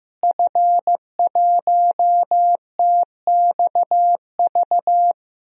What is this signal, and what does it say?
Morse code, heard as a keyed tone, about 15 wpm, 700 Hz. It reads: F1TXV
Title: DXCC-100 Call Signs - 5 Characters - Repeated 2x - 15wpm